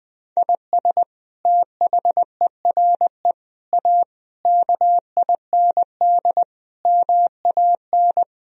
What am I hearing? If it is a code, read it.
Morse code, heard as a keyed tone, about 20 wpm, 700 Hz. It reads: IS THERE A KIND MAN
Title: Sentences from Top 1000 Words - Review Part 3 - 20wpm